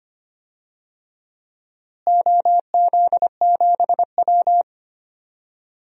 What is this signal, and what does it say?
Morse code, heard as a keyed tone, about 25 wpm, 700 Hz. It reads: OZ7W
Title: DXCC-100 Call Signs - 4 Characters - Repeated 3x - 25wpm